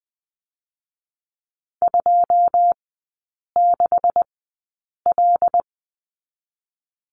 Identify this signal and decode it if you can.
Morse code, heard as a keyed tone, about 20 wpm, spaced as for 8 wpm, 700 Hz. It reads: 26L
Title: 127 Lesson 21e - Number 6 - Three Characters 20wpm